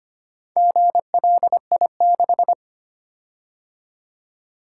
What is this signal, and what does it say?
Morse code, heard as a keyed tone, about 25 wpm, 700 Hz. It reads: GLI6
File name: practice / 25wpm / Four Letter-Number Combinations - 25wpm